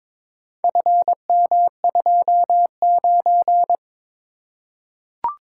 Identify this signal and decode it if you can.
Morse code, heard as a keyed tone, about 22 wpm, 700 Hz. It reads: FM29
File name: practice / 22wpm / Four Letter-Number Combinations - 22wpm